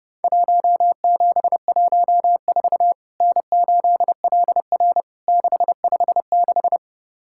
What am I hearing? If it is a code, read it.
Morse code, heard as a keyed tone, about 30 wpm, 700 Hz. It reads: 1714 N8LR 656
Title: Worked All Europe DX Contest - QTC Exchange - 30wpm